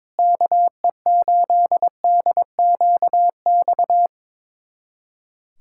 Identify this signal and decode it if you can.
Morse code, heard as a keyed tone, about 22 wpm, 700 Hz. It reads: KE8DQX
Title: USA Callsign DE Callsign 22wpm